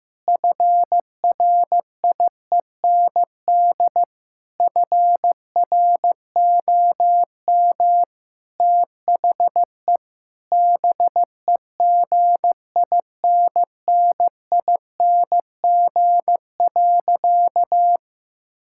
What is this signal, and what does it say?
Morse code, heard as a keyed tone, about 15 wpm, 700 Hz. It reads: FRIEND FROM THE BEGINNING.
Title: Sentences from Top 1000 Words - Review Part 5 - 15wpm